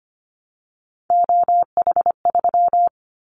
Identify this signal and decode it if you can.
Morse code, heard as a keyed tone, about 25 wpm, 700 Hz. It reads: OH3
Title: Callsign Prefixes for Top 25 DXCC - 25wpm